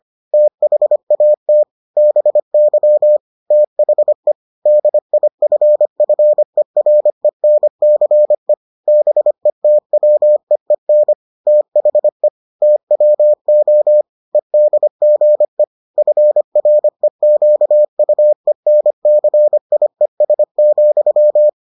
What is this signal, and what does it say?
Morse code, heard as a keyed tone, about 25 wpm, 600 Hz. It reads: THAT BY THE DIFFERENCE BETWEEN THE TWO EDGE FREQUENCIES,